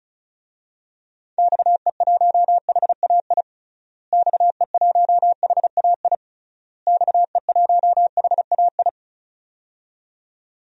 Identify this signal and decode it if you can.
Morse code, heard as a keyed tone, about 36 wpm, 700 Hz. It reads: XE1HAI XE1HAI XE1HAI
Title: DXCC-100 Call Signs - 6 Characters - Repeated 3x - 35wpm